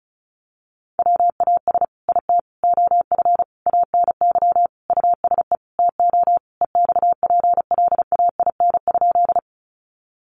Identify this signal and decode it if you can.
Morse code, heard as a keyed tone, about 34 wpm, 700 Hz. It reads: WAS IT OF ANY USE TO EXPLAIN?
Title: Sentences from Top 1100 Words 35wpm